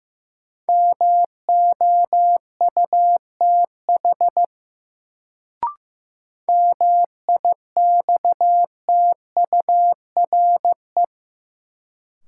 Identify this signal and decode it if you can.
Morse code, heard as a keyed tone, about 15 wpm, 700 Hz. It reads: MOUTH MIXTURE
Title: Sentences from Top 1500 Words - New Word Drill - 15wpm